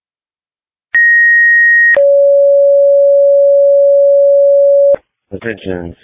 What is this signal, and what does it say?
Two-tone page, heard as a one-second tone, then a three-second tone, about 1900 then 560 Hz, with speech. It is heard from 0.9 to 5.0 s.